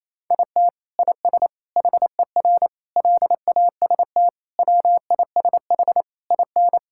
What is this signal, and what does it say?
Morse code, heard as a keyed tone, about 27 wpm, 700 Hz. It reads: IT IS HER LAST WISH IN
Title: Sentences from Top 1600 Words 28wpm